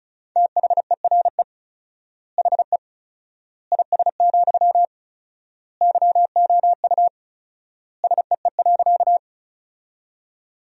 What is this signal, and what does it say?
Morse code, heard as a keyed tone, about 34 wpm, 700 Hz. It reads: THERE HE IS, YOU SEE.
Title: 4x Word-Spacing - Sentences from Top 100 Words - 35wpm